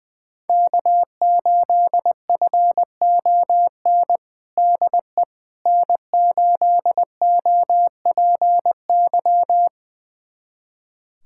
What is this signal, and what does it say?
Morse code, heard as a keyed tone, about 21 wpm, 700 Hz. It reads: K8FON DE N8OPY
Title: USA Callsign DE Callsign - 5 Characters - 20wpm